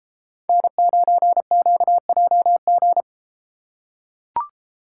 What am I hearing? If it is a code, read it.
Morse code, heard as a keyed tone, about 33 wpm, 700 Hz. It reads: N9QJG